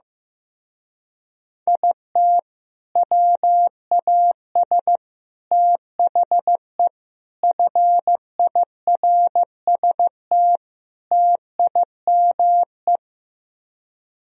Morse code, heard as a keyed tone, about 15 wpm, 700 Hz. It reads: IT WAS THE FIRST TIME
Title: Speed-Racing - Sets of 5 Words 15wpm